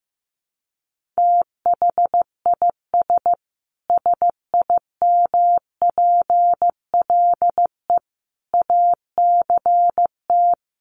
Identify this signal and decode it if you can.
Morse code, heard as a keyed tone, about 15 wpm, 700 Hz. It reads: THIS SIMPLE ACT
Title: Sentences from Top 1500 Words 15wpm